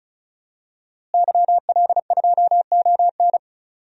Morse code, heard as a keyed tone, about 34 wpm, 700 Hz. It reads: YL2ON
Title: DXCC-100 Call Signs - 5 Characters - Repeated 2x - 35wpm